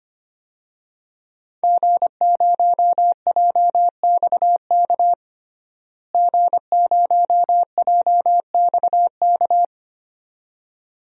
Morse code, heard as a keyed tone, about 25 wpm, 700 Hz. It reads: G0JXK G0JXK
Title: DXCC-100 Call Signs - 5 Characters - Repeated 2x - 25wpm